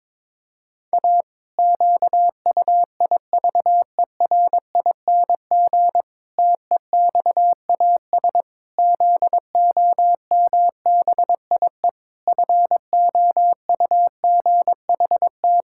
Morse code, heard as a keyed tone, about 22 wpm, 700 Hz. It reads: A QUIVERING TEXAS ZOMBIE FOUGHT
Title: Pangram Challenge v2 - 22wpm